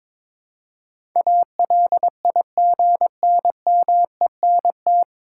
Morse code, heard as a keyed tone, about 22 wpm, 700 Hz. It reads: ALIGNMENT
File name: 9 Letter Words - First 3 Letters Given - 22wpm